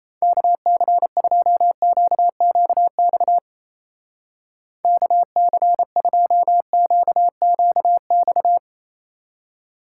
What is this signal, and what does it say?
Morse code, heard as a keyed tone, about 33 wpm, 700 Hz. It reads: KC2QQX KC2QQX
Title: Speed Racing - USA 2x3 Call Signs 28wpm